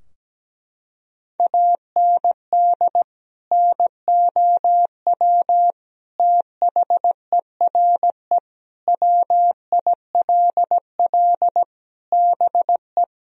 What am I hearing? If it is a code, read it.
Morse code, heard as a keyed tone, about 17 wpm, 700 Hz. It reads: AND NOW THERE WILL BE
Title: Sentences from Top 1000 Words - Review Part 2 - 17wpm